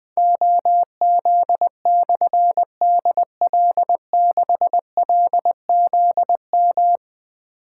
Morse code, heard as a keyed tone, about 20 wpm, 700 Hz. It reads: OZ/DL6LZM
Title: Difficult Call Signs 20wpm